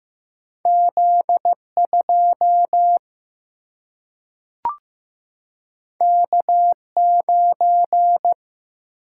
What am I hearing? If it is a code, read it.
Morse code, heard as a keyed tone, about 15 wpm, 700 Hz. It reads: Z 2 K 9
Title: Single-Letter Single-Number 15wpm